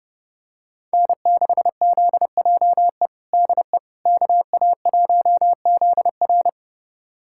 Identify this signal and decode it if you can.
Morse code, heard as a keyed tone, about 32 wpm, 700 Hz. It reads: N6ZJE DE KA1ZR
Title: USA Callsign DE Callsign - 5 Characters - 30wpm